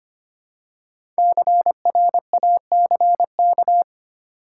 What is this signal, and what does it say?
Morse code, heard as a keyed tone, about 25 wpm, 700 Hz. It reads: CRACK
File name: 5 Letter Words - First 3 Letters Given - 25wpm